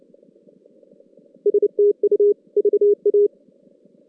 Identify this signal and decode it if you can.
Morse code, heard as a keyed tone, about 27 wpm, 410 Hz. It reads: STU VA
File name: 2016 JAN CW NAQP